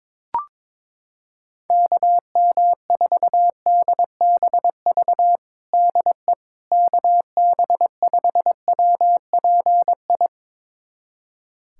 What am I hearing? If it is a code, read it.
Morse code, heard as a keyed tone, about 22 wpm, 700 Hz. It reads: KM4DBV DE KB5WPI
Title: USA Callsign DE Callsign - 6 Characters - 22wpm